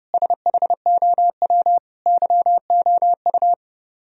Morse code, heard as a keyed tone, about 36 wpm, 700 Hz. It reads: SHOW YOU
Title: Speed-Racing - Sets of 5 Words 30wpm